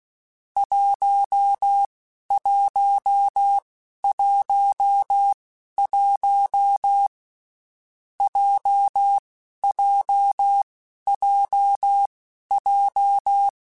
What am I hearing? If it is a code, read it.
Morse code, heard as a keyed tone, about 16 wpm, spaced as for 11 wpm, 800 Hz. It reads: 1111 JJJJ
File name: Basic Sets / Basic Letter Groups Collective7 / JZ17@10